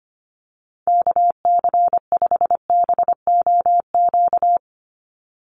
Morse code, heard as a keyed tone, about 25 wpm, 690 Hz. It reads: KC5BOQ
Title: USA Call Signs 25wpm